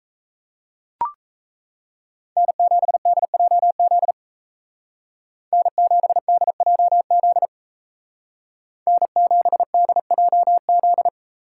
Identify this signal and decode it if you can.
Morse code, heard as a keyed tone, about 36 wpm, 700 Hz. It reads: N7DJZ N7DJZ N7DJZ